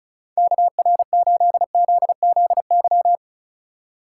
Morse code, heard as a keyed tone, about 36 wpm, 700 Hz. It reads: KR8ZZY
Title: USA Call Signs - 3x Character Spacing - 35wpm